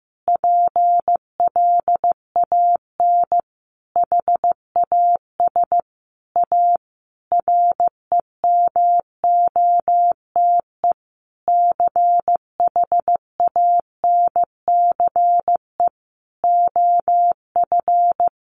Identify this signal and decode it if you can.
Morse code, heard as a keyed tone, about 15 wpm, 700 Hz. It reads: PLAN HAS A REMOTE CHANCE OF
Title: Sentences from Top 1300 Words 15wpm